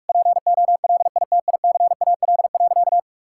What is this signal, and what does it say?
Morse code, heard as a keyed tone, about 41 wpm, 700 Hz. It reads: POLITICAL.
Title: Sentences from Top 1600 Words 45wpm